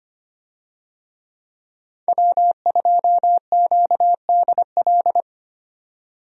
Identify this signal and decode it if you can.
Morse code, heard as a keyed tone, about 25 wpm, 700 Hz. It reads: W2QDL
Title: USA Call Signs - 5 Characters - 25wpm